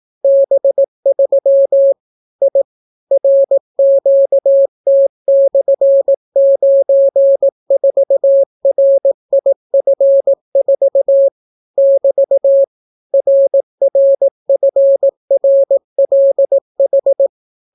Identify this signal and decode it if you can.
Morse code, heard as a keyed tone, about 18 wpm, 550 Hz. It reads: B3 I RQT/94RIF4 <BT> RRFRLH